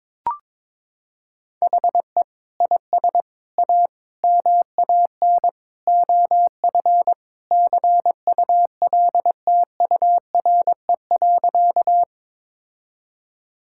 Morse code, heard as a keyed tone, about 23 wpm, 700 Hz. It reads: HE IS A MAN OF CULTURE.
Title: Sentences from Top 1000 Words - Review Part 6 - 22wpm